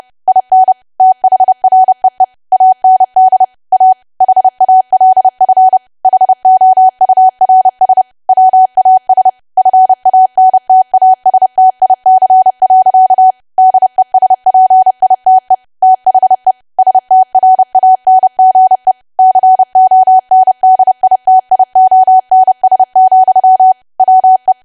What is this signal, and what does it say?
Morse code, heard as a keyed tone, about 30 wpm, 750 Hz. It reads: IN THREE AND A HALF HOURS WAS FANTASTIC. DESPITE THE STRANGE CONDITIONS, WE
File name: cwa / W1AW-Level3 / 30WPM